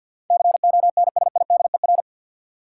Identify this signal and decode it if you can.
Morse code, heard as a keyed tone, about 46 wpm, 700 Hz. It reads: CONSIDER
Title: Sentences from Top 600 Words - New Word Drill - 50wpm